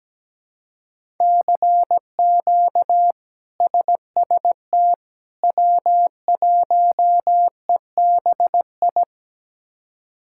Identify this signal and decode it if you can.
Morse code, heard as a keyed tone, about 17 wpm, 700 Hz. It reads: CQ SST W1EBI
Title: K1USN SST QSO Practice - 17wpm